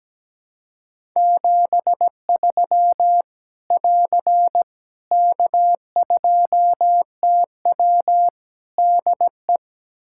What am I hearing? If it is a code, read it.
Morse code, heard as a keyed tone, about 17 wpm, 700 Hz. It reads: 73 <AR> K2TW DE
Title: K1USN SST QSO Practice 17wpm